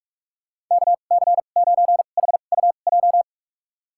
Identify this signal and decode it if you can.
Morse code, heard as a keyed tone, about 46 wpm, spaced as for 32 wpm, 700 Hz. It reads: KC9HUJ
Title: USA Call Signs - 6 Characters - 2x Character Spacing - 45wpm